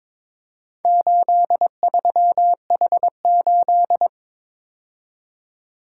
Morse code, heard as a keyed tone, about 22 wpm, 700 Hz. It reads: 83H8
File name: practice / 22wpm / Four Letter-Number Combinations - 22wpm